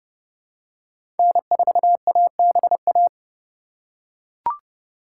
Morse code, heard as a keyed tone, about 30 wpm, 700 Hz. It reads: N4ABA
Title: USA Call Signs - 5 Characters - 3x Character Spacing - 30wpm